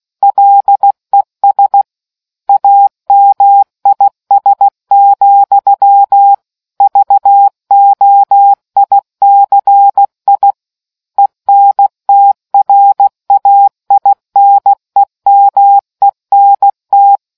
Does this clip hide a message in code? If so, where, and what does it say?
Yes: Morse code from 0.2 to 17.2 s, heard as a keyed tone, about 16 wpm, 800 Hz. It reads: LES AMIS, VOICI ENTRAINEMENT